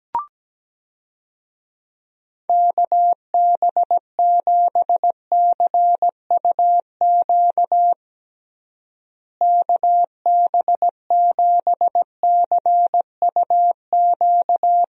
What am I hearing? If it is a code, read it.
Morse code, heard as a keyed tone, about 17 wpm, 700 Hz. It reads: KB7CUQ KB7CUQ
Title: USA Call Signs - 6 Characters - Repeated 3x - 17wpm